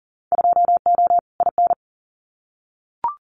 Morse code, heard as a keyed tone, about 40 wpm, 700 Hz. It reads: 2O IN
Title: Winter Field Day Exchanges 40wpm